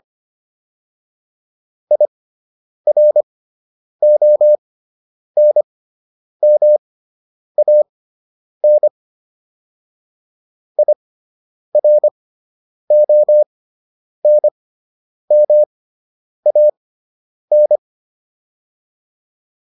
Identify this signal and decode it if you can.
Morse code, heard as a keyed tone, about 25 wpm, spaced as for 9 wpm, 600 Hz. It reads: IRONMAN IRONMAN